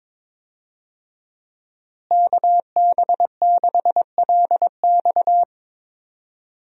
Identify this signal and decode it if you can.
Morse code, heard as a keyed tone, about 22 wpm, 700 Hz. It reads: KB6LX
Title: USA Call Signs - 5 Characters - Repeated 2x - 22wpm